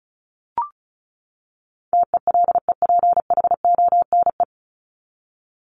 Morse code, heard as a keyed tone, about 34 wpm, 700 Hz. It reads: TELEPHONE
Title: Top 1501-1750 Words 35wpm